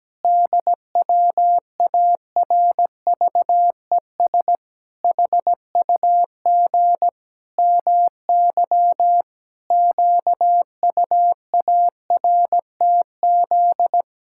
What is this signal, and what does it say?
Morse code, heard as a keyed tone, about 17 wpm, 700 Hz. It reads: DWARVES HUG MY QUARTZ
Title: Pangram Challenge v2 - 17wpm